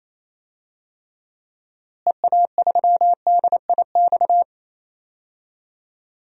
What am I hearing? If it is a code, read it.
Morse code, heard as a keyed tone, about 28 wpm, 700 Hz. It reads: EA3DIX
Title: DXCC-100 Call Signs - 6 Characters - Repeated 3x - 28wpm